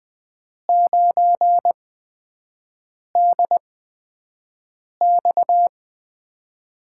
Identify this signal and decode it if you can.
Morse code, heard as a keyed tone, about 20 wpm, spaced as for 5 wpm, 700 Hz. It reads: 9DX
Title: USA Call Signs - 5 Characters - 8x Character Spacing - 20wpm